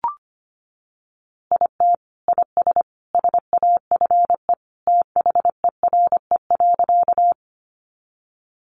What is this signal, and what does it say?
Morse code, heard as a keyed tone, about 25 wpm, 700 Hz. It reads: IT IS SAFE THERE.
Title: Easy Phrases 25wpm